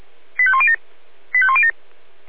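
Five-tone selective call (ZVEI), heard as quick runs of five tones, tones of about 70 ms. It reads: 85107 85107
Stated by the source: ZVEI five-tone call received by SDR.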